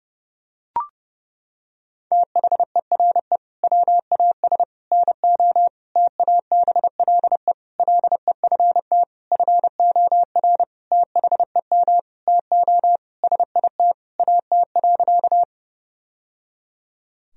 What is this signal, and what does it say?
Morse code, heard as a keyed tone, about 30 wpm, 700 Hz. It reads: THERE WAS NO TABLE LEFT FOR THEM TO SIT AT.
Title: Sentences from Top 1000 Words - Review Part 2 - 30wpm